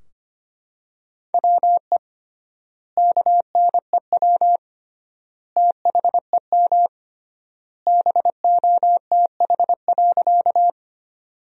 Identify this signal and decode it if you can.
Morse code, heard as a keyed tone, about 25 wpm, 700 Hz. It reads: WE KNEW THEM BOTH.
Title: 3x Word-Spacing - Sentences from Top 400 Words - 25wpm